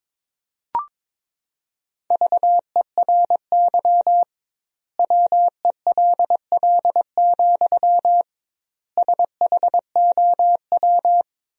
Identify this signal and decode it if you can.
Morse code, heard as a keyed tone, about 22 wpm, 700 Hz. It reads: VERY WELL, SHOW
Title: 2x Word-Spacing - Sentences from Top 200 Words - 22wpm